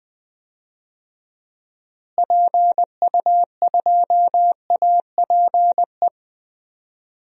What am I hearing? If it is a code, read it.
Morse code, heard as a keyed tone, about 20 wpm, 700 Hz. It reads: PU2APE
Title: DXCC-100 Call Signs - 6 Characters - Repeated 3x - 20wpm